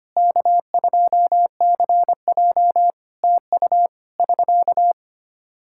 Morse code, heard as a keyed tone, about 25 wpm, 700 Hz. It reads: K2CJ TU <SK>
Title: K1USN SST QSO Practice 25wpm